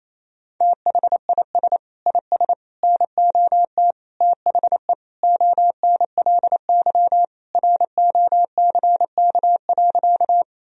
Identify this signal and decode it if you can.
Morse code, heard as a keyed tone, about 28 wpm, 700 Hz. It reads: THIS IS NOT THE ONLY ROCK.
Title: Sentences from Top 1000 Words - Review Part 2 - 28wpm